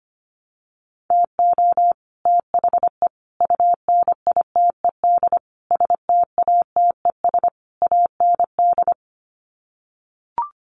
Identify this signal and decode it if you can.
Morse code, heard as a keyed tone, about 25 wpm, 700 Hz. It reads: TO THE UNITED STATES AND
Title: Sets of 5 Words Encore 25wpm